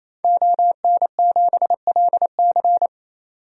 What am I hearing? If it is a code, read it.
Morse code, heard as a keyed tone, about 28 wpm, 700 Hz. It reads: ON7LC
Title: DXCC-100 Call Signs - 5 Characters - 28wpm